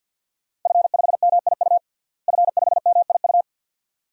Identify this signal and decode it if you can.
Morse code, heard as a keyed tone, about 51 wpm, 700 Hz. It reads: F5MIU F5MIU
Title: DXCC-100 Call Signs - 5 Characters - Repeated 2x - 50wpm